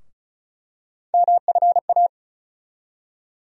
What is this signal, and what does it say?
Morse code, heard as a keyed tone, about 36 wpm, 700 Hz. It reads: MFA
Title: USA Call Sign Postfixes - 3 Letters 35wpm